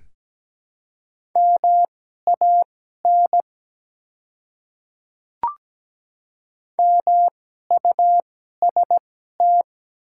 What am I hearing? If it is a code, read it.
Morse code, heard as a keyed tone, about 17 wpm, spaced as for 12 wpm, 700 Hz. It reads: MAN MUST
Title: 2x Character Spacing - Top 200 Words - 17wpm